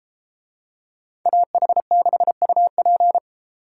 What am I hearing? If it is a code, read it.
Morse code, heard as a keyed tone, about 32 wpm, 700 Hz. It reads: AH6UP